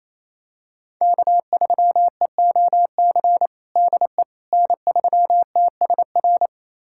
Morse code, heard as a keyed tone, about 28 wpm, 700 Hz. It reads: K3EOC DE N3TSR
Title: USA Callsign DE Callsign - 5 Characters - 28wpm